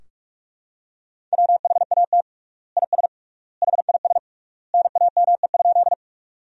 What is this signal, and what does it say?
Morse code, heard as a keyed tone, about 46 wpm, 700 Hz. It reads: WHAT IS HIS NAME?
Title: 3x Word-Spacing - Sentences from Top 200 Words - 45wpm